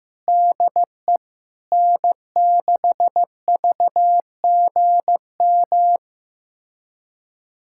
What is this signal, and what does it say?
Morse code, heard as a keyed tone, about 15 wpm, 700 Hz. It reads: DE N6VGM
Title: USA Callsign DE Callsign - 5 Characters - 15wpm